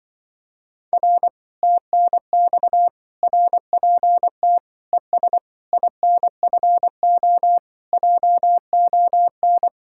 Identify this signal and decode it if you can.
Morse code, heard as a keyed tone, about 24 wpm, 700 Hz. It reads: R TNX RPT ES INFO JON
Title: Speed-Racing Common QSO Phrases - 20wpm